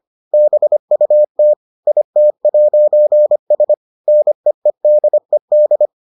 Morse code, heard as a keyed tone, about 25 wpm, 600 Hz. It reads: BUT IT'S NEEDED